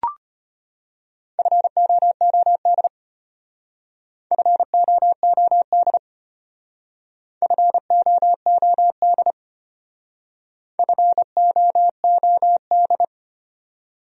Morse code, heard as a keyed tone, about 33 wpm, 700 Hz. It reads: FOOD FOOD FOOD FOOD